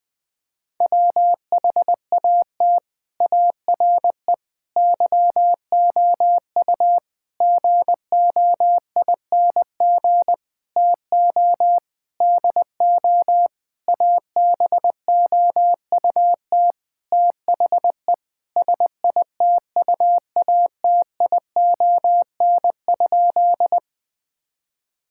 Morse code, heard as a keyed tone, about 20 wpm, 700 Hz. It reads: WHAT ARE YOU GOING TO DO ABOUT THE SITUATION?
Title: Sentences with 8-Word Phrases - 20wpm